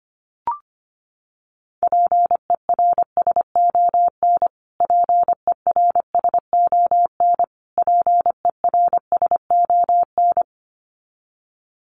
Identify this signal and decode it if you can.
Morse code, heard as a keyed tone, about 25 wpm, 700 Hz. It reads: PERSON PERSON PERSON
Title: Top 500 Words Repeated 25wpm